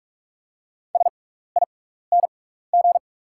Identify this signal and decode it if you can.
Morse code, heard as a keyed tone, about 44 wpm, spaced as for 15 wpm, 690 Hz. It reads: SING
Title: 6x Character Spacing - Top 500 Words - 45wpm